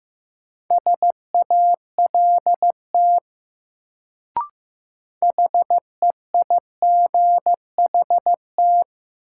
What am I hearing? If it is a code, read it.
Morse code, heard as a keyed tone, about 15 wpm, 700 Hz. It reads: SALT HEIGHT
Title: Sets of 5 Words - New Word Drill - 15wpm